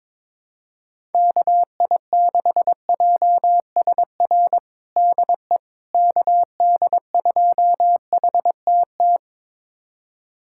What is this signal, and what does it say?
Morse code, heard as a keyed tone, about 22 wpm, 700 Hz. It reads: KI6JSR DE KD2HTT
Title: USA Callsign DE Callsign - 6 Characters - 22wpm